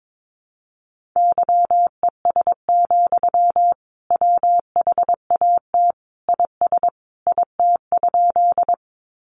Morse code, heard as a keyed tone, about 22 wpm, 700 Hz. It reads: YES, WHAT IS IT?
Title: Sentences from Top 500 Words 22wpm